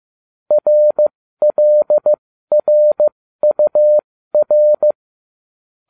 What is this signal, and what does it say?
Morse code, heard as a keyed tone, about 15 wpm, spaced as for 13 wpm, 600 Hz. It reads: RLRUR